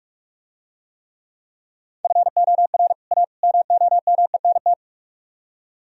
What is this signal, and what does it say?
Morse code, heard as a keyed tone, about 47 wpm, 700 Hz. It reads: FOR A MOMENT